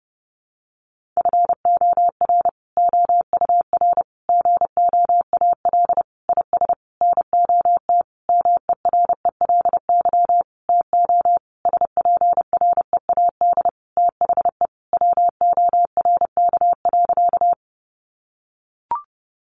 Morse code, heard as a keyed tone, about 30 wpm, 700 Hz. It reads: FOR OUR GOAL IS NOT MERELY TO SPREAD THE WORK.
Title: Sentences from Top 1400 Words 30wpm